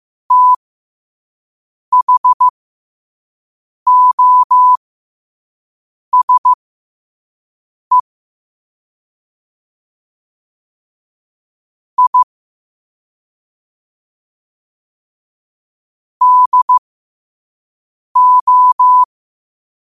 Morse code, heard as a keyed tone, about 15 wpm, spaced as for 5 wpm, 1000 Hz. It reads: THOSE I DO
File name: qotdfiles / 5wpm